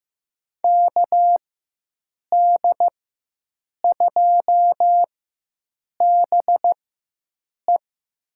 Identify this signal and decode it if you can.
Morse code, heard as a keyed tone, about 15 wpm, spaced as for 7 wpm, 700 Hz. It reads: KD2BE
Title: USA Call Signs - 5 Characters - 4x Character Spacing - 15wpm